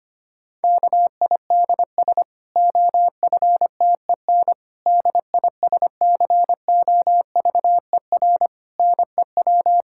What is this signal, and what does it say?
Morse code, heard as a keyed tone, about 25 wpm, 700 Hz. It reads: KIDS OFTEN DISCOVER NEW
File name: practice / 25wpm / Difficult Sentences with Hints - 25wpm